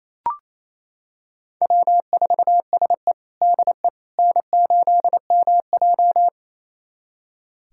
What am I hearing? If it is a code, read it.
Morse code, heard as a keyed tone, about 28 wpm, 700 Hz. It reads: W4SE DE N8MJ
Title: USA Callsign DE Callsign - 4 Characters - 28wpm